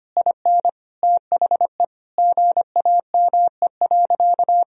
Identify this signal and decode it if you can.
Morse code, heard as a keyed tone, about 25 wpm, 700 Hz. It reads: IN THE GAME.
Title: Sentences from Top 1000 Words - Review Part 3 - 25wpm